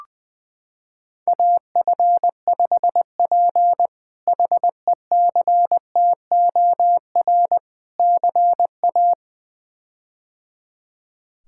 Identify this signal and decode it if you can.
Morse code, heard as a keyed tone, about 20 wpm, 700 Hz. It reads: AF5P HECTOR CA
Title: K1USN SST QSO Practice - 20wpm